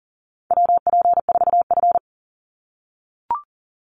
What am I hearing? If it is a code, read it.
Morse code, heard as a keyed tone, about 40 wpm, 700 Hz. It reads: WP4F
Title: Call Signs - 40wpm